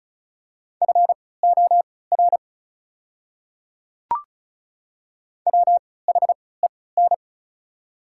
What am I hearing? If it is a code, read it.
Morse code, heard as a keyed tone, about 36 wpm, spaced as for 20 wpm, 690 Hz. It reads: FOR WHEN
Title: 3x Character Spacing - Top 100 Words - 35wpm